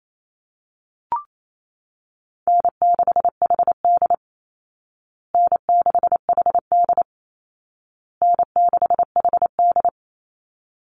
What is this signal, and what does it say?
Morse code, heard as a keyed tone, about 28 wpm, 700 Hz. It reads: N6HD N6HD N6HD
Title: Call Signs Repeated 2x - 28wpm